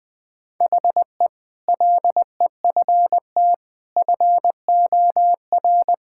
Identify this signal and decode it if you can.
Morse code, heard as a keyed tone, about 21 wpm, 700 Hz. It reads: HE LEFT FOR
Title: Sentences with 8-Word Phrases - Spoken Beginning - 20wpm